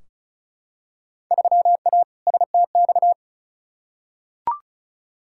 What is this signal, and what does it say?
Morse code, heard as a keyed tone, about 36 wpm, 700 Hz. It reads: 3A STX
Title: Field Day Exchanges - 35wpm